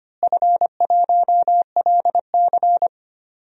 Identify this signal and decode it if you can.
Morse code, heard as a keyed tone, about 25 wpm, 700 Hz. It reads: F1LC
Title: DXCC-100 Call Signs - 4 Characters - Repeated 3x - 25wpm